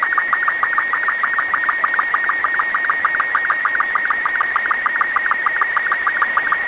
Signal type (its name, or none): SSTV (slow-scan television)